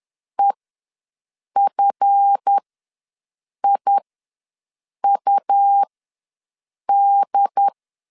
Morse code, heard as a keyed tone, about 10 wpm, spaced as for 6 wpm, 800 Hz. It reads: EFIUD
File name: LEZIONI / 8 YJ45 / 8 YJ45 ws 10wpm 3843